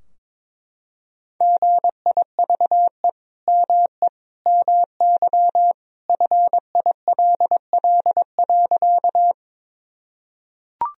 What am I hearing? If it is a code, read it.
Morse code, heard as a keyed tone, about 22 wpm, 700 Hz. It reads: GIVE ME MY FILL.
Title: Sentences from Top 1000 Words - Review Part 3 - 22wpm